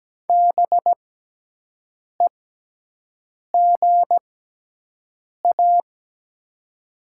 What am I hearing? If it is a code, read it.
Morse code, heard as a keyed tone, about 17 wpm, spaced as for 6 wpm, 700 Hz. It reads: BEGA
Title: 6x Character Spacing - Top 500 Words - 17wpm